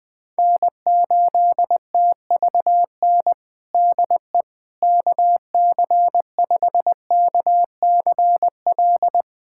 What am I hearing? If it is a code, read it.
Morse code, heard as a keyed tone, about 20 wpm, 700 Hz. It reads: N8TVN DE KC5KCL
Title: USA Callsign DE Callsign 20wpm